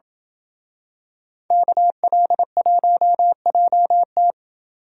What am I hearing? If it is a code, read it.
Morse code, heard as a keyed tone, about 28 wpm, 700 Hz. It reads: KL1JT